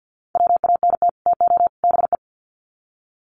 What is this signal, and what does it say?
Morse code, heard as a keyed tone, about 50 wpm, 700 Hz. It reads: WANT TO BE
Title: Sets of 3 Words 50wpm